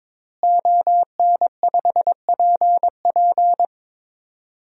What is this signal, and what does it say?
Morse code, heard as a keyed tone, about 22 wpm, 700 Hz. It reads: ON5PP
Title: DXCC-100 Call Signs - 5 Characters - Repeated 2x - 22wpm